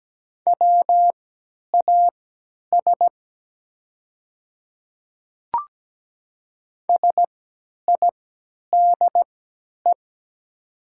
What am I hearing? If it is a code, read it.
Morse code, heard as a keyed tone, about 17 wpm, spaced as for 10 wpm, 700 Hz. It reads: WAS SIDE
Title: 3x Character Spacing - Top 100 Words - 17wpm